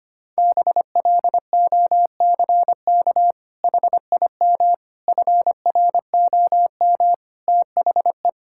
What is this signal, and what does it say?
Morse code, heard as a keyed tone, about 25 wpm, 700 Hz. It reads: BLOCK HIM FROM THE
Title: Sentences from Top 1600 Words 25wpm